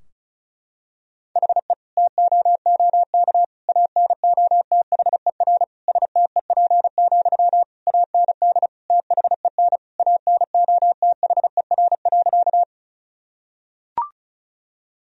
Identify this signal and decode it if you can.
Morse code, heard as a keyed tone, about 34 wpm, 700 Hz. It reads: HE TOOK ANOTHER STEP, AND THEN ANOTHER.
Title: Sentences from Top 1000 Words - Review Part 2 - 35wpm